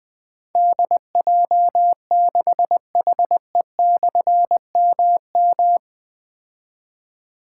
Morse code, heard as a keyed tone, about 20 wpm, 700 Hz. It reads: DJ6HE/MM
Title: Difficult Call Signs 20wpm